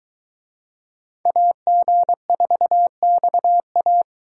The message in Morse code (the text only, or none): AG4XA